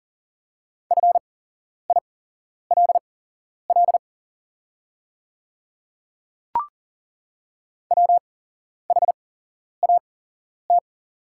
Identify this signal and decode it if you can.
Morse code, heard as a keyed tone, about 40 wpm, spaced as for 11 wpm, 700 Hz. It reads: FILL WHAT
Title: 8x Character Spacing - Top 500 Words - 40wpm